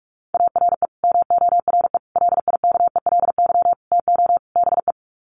Morse code, heard as a keyed tone, about 44 wpm, 700 Hz. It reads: ARE MORE LIKELY TO BE
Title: Sets of 5 Words 45wpm